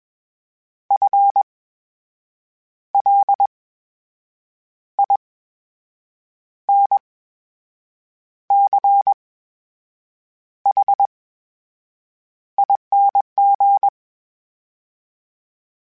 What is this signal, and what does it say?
Morse code, heard as a keyed tone, about 20 wpm, spaced as for 6 wpm, 800 Hz. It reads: FLINCHING
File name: Special INGs-mp3s / Special INGs-mp3s / VerySlowToFaster Ing @20